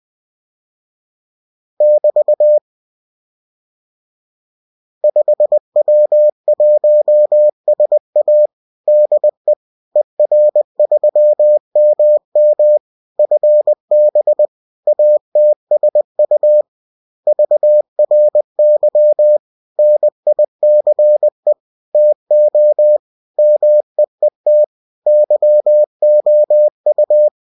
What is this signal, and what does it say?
Morse code, heard as a keyed tone, about 20 wpm, 600 Hz. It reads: <BT> 5W1SA DE ER3MM FB ATSU VRY NICE TO MEET YOU